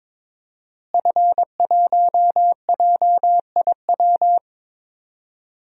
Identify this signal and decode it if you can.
Morse code, heard as a keyed tone, about 22 wpm, 700 Hz. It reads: F1JIW
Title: DXCC-100 Call Signs - 5 Characters - Repeated 3x - 22wpm